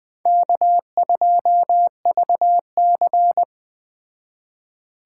Morse code, heard as a keyed tone, about 21 wpm, 700 Hz. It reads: K2VC
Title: Speed Racing - USA 1x2 Call Signs 17wpm